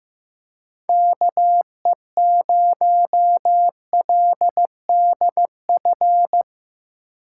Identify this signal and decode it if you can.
Morse code, heard as a keyed tone, about 16 wpm, 700 Hz. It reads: KE0LDF
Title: Speed Racing - USA 2x3 Call Signs 15wpm